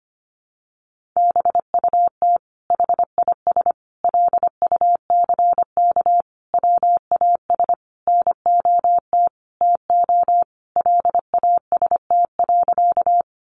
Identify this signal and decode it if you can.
Morse code, heard as a keyed tone, about 25 wpm, 700 Hz. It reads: BUT HIS LUCK WAS NOT TO LAST.
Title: Sentences from Top 1500 Words 25wpm